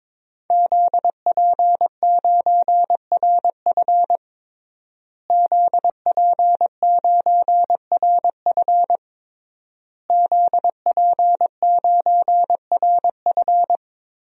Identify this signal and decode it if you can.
Morse code, heard as a keyed tone, about 22 wpm, 700 Hz. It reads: ZP9RF ZP9RF ZP9RF
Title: DXCC-100 Call Signs - 5 Characters - Repeated 3x - 22wpm